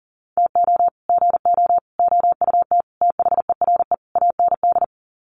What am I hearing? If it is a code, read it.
Morse code, heard as a keyed tone, about 37 wpm, 700 Hz. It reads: TO GO OUT THERE AND
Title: Sets of 5 Words Encore2 40wpm